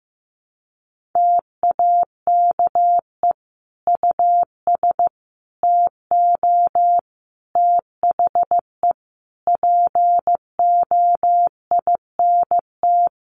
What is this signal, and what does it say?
Morse code, heard as a keyed tone, about 15 wpm, 700 Hz. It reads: TAKE US TO THE POINT
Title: Sentences from Top 1500 Words 15wpm